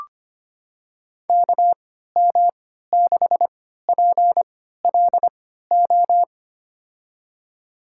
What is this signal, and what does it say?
Morse code, heard as a keyed tone, about 25 wpm, spaced as for 14 wpm, 700 Hz. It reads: KM6PLO
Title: USA Call Signs - 3x Character Spacing - 25wpm